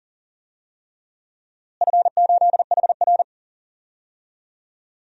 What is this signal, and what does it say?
Morse code, heard as a keyed tone, about 40 wpm, 700 Hz. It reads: F8HR
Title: DXCC-100 Call Signs - 4 Characters - Repeated 3x - 40wpm